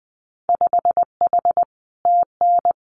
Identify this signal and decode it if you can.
Morse code, heard as a keyed tone, about 20 wpm, 700 Hz. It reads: 5H TN
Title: Winter Field Day Exchanges Encore 20wpm